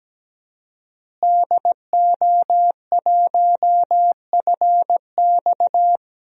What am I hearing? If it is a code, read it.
Morse code, heard as a keyed tone, about 17 wpm, 700 Hz. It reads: DO1FX
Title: DXCC-100 Call Signs - 5 Characters - Repeated 3x - 17wpm